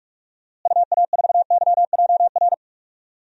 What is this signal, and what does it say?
Morse code, heard as a keyed tone, about 41 wpm, 700 Hz. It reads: UA4YJR
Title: DXCC-100 Call Signs - 6 Characters - Repeated 3x - 45wpm